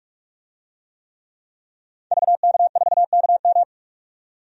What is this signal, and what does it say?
Morse code, heard as a keyed tone, about 42 wpm, 700 Hz. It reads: VK4KM
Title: DXCC-100 Call Signs - 5 Characters - 45wpm